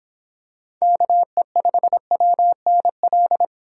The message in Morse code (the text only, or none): KE5WNL